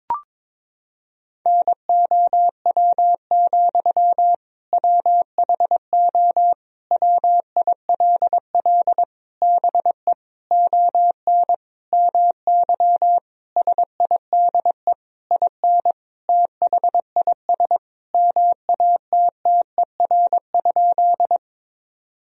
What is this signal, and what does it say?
Morse code, heard as a keyed tone, about 22 wpm, 700 Hz. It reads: NOW, WHO WILL BE ON MY SIDE IN THIS MATTER?
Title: Sentences from Top 1000 Words - Review Part 1 - 22wpm